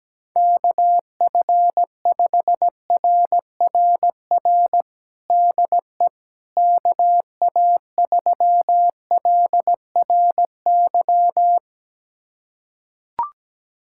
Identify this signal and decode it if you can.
Morse code, heard as a keyed tone, about 17 wpm, 700 Hz. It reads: KF5RRR DE KA3LRY
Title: USA Callsign DE Callsign - 6 Characters - 17wpm